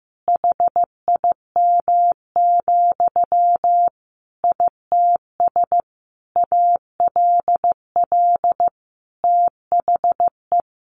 Morse code, heard as a keyed tone, about 15 wpm, 700 Hz. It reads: HIM, ITS ALL THE
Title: Sentences from Top 1000 Words 15wpm